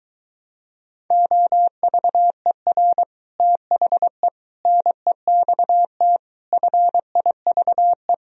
Morse code, heard as a keyed tone, about 23 wpm, 700 Hz. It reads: OVER THE NEXT FIVE